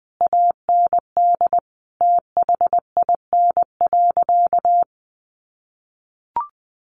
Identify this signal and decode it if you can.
Morse code, heard as a keyed tone, about 20 wpm, 700 Hz. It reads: AND THIN.
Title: Sentences from Top 1200 Words 20wpm